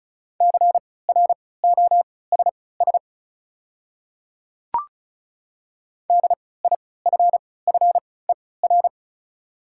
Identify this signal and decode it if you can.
Morse code, heard as a keyed tone, about 36 wpm, spaced as for 20 wpm, 700 Hz. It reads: CROSS DIFFER
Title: 3x Character Spacing - Top 500 Words - 35wpm